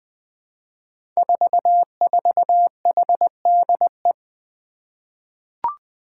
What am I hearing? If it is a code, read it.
Morse code, heard as a keyed tone, about 20 wpm, 700 Hz. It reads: 44HDE
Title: Five Challenging Letter-Number Combinations 20wpm